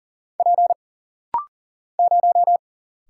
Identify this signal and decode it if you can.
Morse code, heard as a keyed tone, about 40 wpm, 700 Hz. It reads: P 0
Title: Single Letter-Number Alt - Rapid-Fire 40wpm